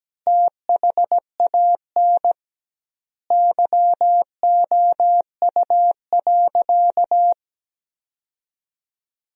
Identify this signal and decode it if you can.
Morse code, heard as a keyed tone, about 17 wpm, 700 Hz. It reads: THAN YOU.
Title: 2x Word-Spacing - Sentences from Top 400 Words - 17wpm